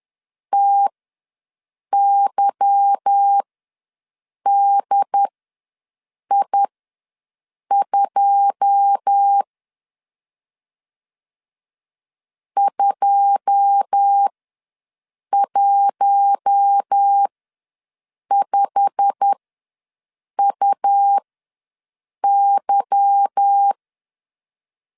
Morse code, heard as a keyed tone, about 10 wpm, spaced as for 6 wpm, 800 Hz. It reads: TYDI2 215UY
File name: LEZIONI / 9 2 ws 10wpm 4130